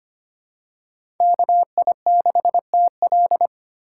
Morse code, heard as a keyed tone, about 25 wpm, 700 Hz. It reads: KI6TL
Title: USA Call Signs - 5 Characters - 2x Character Spacing - 25wpm